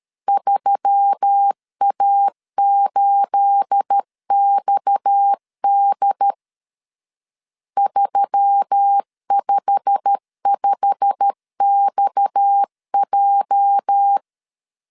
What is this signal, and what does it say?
Morse code, heard as a keyed tone, about 13 wpm, 800 Hz. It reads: 3A8XD 355XJ
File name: CW / DET / 1 9 1500C S C 12wpm 3811